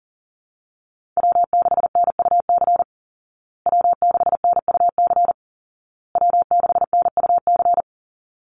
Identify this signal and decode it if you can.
Morse code, heard as a keyed tone, about 40 wpm, 700 Hz. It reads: W6NUC W6NUC W6NUC
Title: Call Signs Repeated 2x - 40wpm